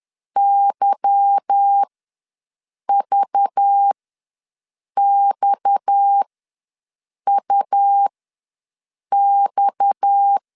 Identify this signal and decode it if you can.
Morse code, heard as a keyed tone, about 10 wpm, spaced as for 6 wpm, 800 Hz. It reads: YVXUX